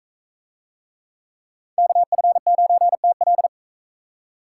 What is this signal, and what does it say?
Morse code, heard as a keyed tone, about 41 wpm, 700 Hz. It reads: KF9TL